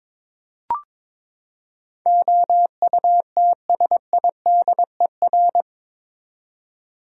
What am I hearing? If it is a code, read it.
Morse code, heard as a keyed tone, about 22 wpm, 700 Hz. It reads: OUTSIDER
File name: practice / 22wpm / Eight Letter Words - 22wpm